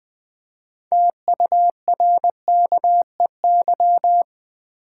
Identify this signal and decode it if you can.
Morse code, heard as a keyed tone, about 20 wpm, 700 Hz. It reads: TURKEY
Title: Easy Words with Hints - 20wpm